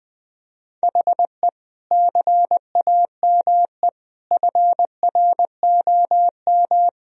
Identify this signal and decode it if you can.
Morse code, heard as a keyed tone, about 20 wpm, 700 Hz. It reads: HE CAME FROM
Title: Sentences from Top 1000 Words - Review Part 1 - 20wpm